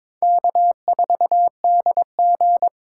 Morse code, heard as a keyed tone, about 21 wpm, 700 Hz. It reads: K4DG
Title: USA Call Signs - 4x Character Spacing - 22wpm